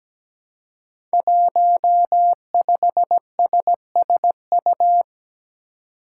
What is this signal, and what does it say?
Morse code, heard as a keyed tone, about 17 wpm, 700 Hz. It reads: 15SSU
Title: Five Challenging Letter-Number Combinations 17wpm